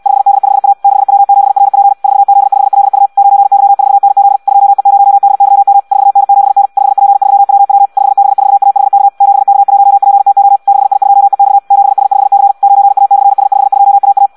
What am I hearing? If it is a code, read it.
Morse code, heard as a keyed tone, about 74 wpm, 800 Hz. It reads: 5D5A 6W7RP 4Z5LU 9Y4NZ 2E0DYM 4N7N 5P3WW 5B4AHK 7K1PTO 6I2EX 7S5C 9A/S52AA